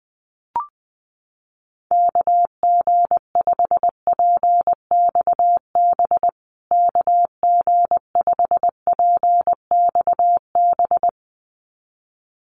Morse code, heard as a keyed tone, about 20 wpm, 700 Hz. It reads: KG5PXB KG5PXB
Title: USA Call Signs Repeated 2x 20wpm